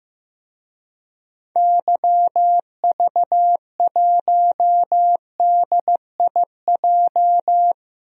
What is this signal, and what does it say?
Morse code, heard as a keyed tone, about 15 wpm, 700 Hz. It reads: YV1DIJ
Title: DXCC-100 Call Signs - 6 Characters - 15wpm